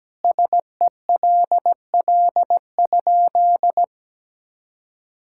Morse code, heard as a keyed tone, about 17 wpm, 700 Hz. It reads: SELL?
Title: Sentences from Top 1600 Words 17wpm